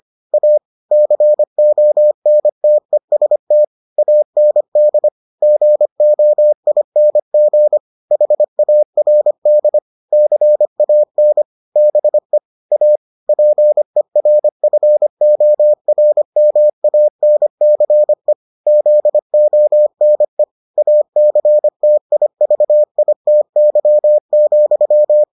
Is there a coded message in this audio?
Yes: Morse code from 0.3 to 25.2 s, heard as a keyed tone, about 25 wpm, 600 Hz. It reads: A CONTEST AND GOING HARD CAN BE A PERFORMANCE ZONE ACTIVITY,